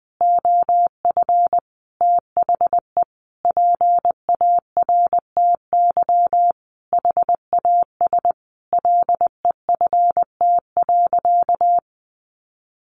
Morse code, heard as a keyed tone, about 20 wpm, 700 Hz. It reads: OF THE PARTY HAS LEFT.
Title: Sentences from Top 1500 Words 20wpm